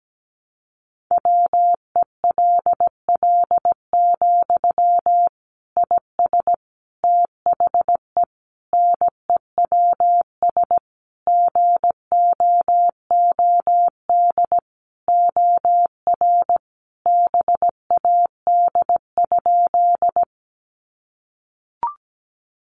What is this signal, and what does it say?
Morse code, heard as a keyed tone, about 17 wpm, 700 Hz. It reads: WELL, IS THE NEWS GOOD OR BAD?
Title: Sentences from Top 800 Words 17wpm